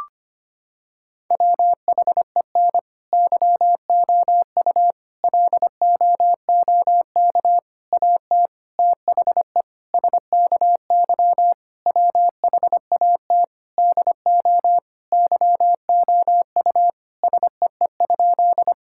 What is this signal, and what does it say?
Morse code, heard as a keyed tone, about 25 wpm, 700 Hz. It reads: WHEN YOU LOOK AT THE SKY WHAT DO YOU SEE?
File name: practice / 25wpm / Sentences with 3-Word Phrases - 25wpm